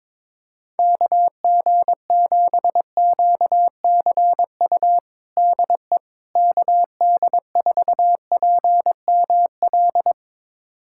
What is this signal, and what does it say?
Morse code, heard as a keyed tone, about 22 wpm, 700 Hz. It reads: KG7QCU DE KD4PML
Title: USA Callsign DE Callsign - 6 Characters - 22wpm